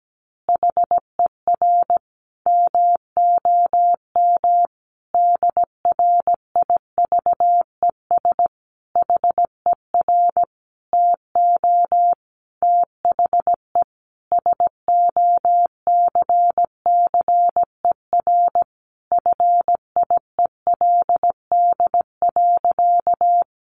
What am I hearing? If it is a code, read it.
Morse code, heard as a keyed tone, about 17 wpm, 700 Hz. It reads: HER MOM DRIVES HER TO THE SOCCER FIELD.
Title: Easy Phrases 17wpm